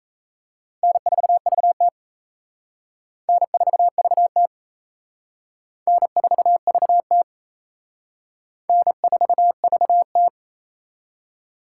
Morse code, heard as a keyed tone, about 37 wpm, 700 Hz. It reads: N4VT N4VT N4VT N4VT